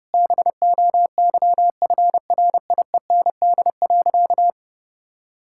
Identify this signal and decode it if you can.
Morse code, heard as a keyed tone, about 30 wpm, 700 Hz. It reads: BOYFRIEND.
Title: Sentences from Top 1000 Words - Review Part 5 - 30wpm